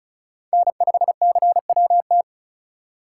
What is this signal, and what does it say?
Morse code, heard as a keyed tone, about 39 wpm, 700 Hz. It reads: N5CWT
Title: Speed Racing - USA 1x3 Call Signs 35wpm